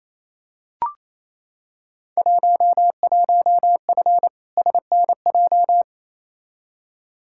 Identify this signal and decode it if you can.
Morse code, heard as a keyed tone, about 28 wpm, 700 Hz. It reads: 11F SNJ
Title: Field Day Exchanges Encore - 28wpm